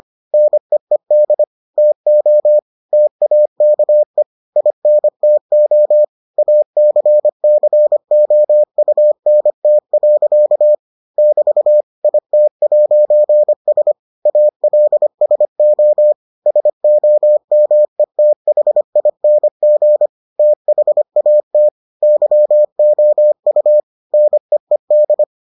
Morse code, heard as a keyed tone, about 25 wpm, 600 Hz. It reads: NEED TO TAKE INTO ACCOUNT. <BT> IT'S ALSO SOMETHING THAT YOU NEED